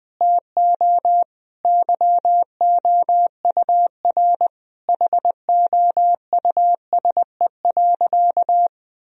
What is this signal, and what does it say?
Morse code, heard as a keyed tone, about 20 wpm, 700 Hz. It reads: TO YOUR HOUSE.
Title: Sentences from Top 1000 Words - Review Part 2 - 20wpm